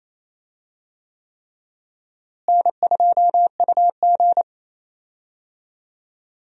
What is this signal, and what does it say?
Morse code, heard as a keyed tone, about 28 wpm, 700 Hz. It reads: N2UG
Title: USA Call Signs - 4 Characters - Repeated 3x - 28wpm